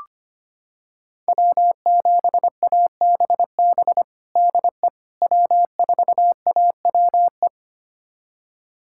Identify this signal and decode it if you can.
Morse code, heard as a keyed tone, about 25 wpm, 700 Hz. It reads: W7ABB DE W4AWE
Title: USA Callsign DE Callsign - 5 Characters - 25wpm